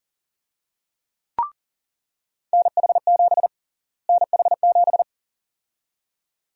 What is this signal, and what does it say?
Morse code, heard as a keyed tone, about 39 wpm, 700 Hz. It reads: NH7 NH7
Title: USA Call Sign Prefixes - Repeated 2x - 40wpm